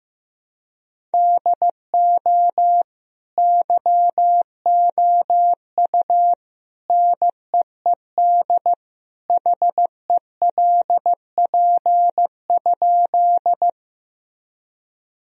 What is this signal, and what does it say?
Morse code, heard as a keyed tone, about 15 wpm, 700 Hz. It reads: DO YOU NEED HELP?
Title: Sentences from Top 1000 Words - Review Part 1 - 15wpm